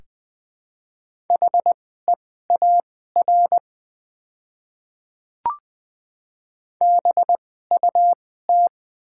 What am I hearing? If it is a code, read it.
Morse code, heard as a keyed tone, about 20 wpm, spaced as for 14 wpm, 700 Hz. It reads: HEAR BUT
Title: 2x Character Spacing - Top 500 Words - 20wpm